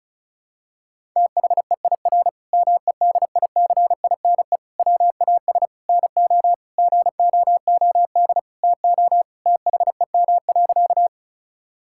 Morse code, heard as a keyed tone, about 34 wpm, 700 Hz. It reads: THEIR MEDICINE WAS NO GOOD TO THEM.
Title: Sentences from Top 1000 Words - Review Part 6 - 35wpm